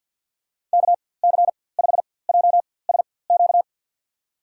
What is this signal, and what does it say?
Morse code, heard as a keyed tone, about 53 wpm, spaced as for 24 wpm, 700 Hz. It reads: KC5JSQ
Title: USA Call Signs - 6 Characters - 4x Character Spacing - 50wpm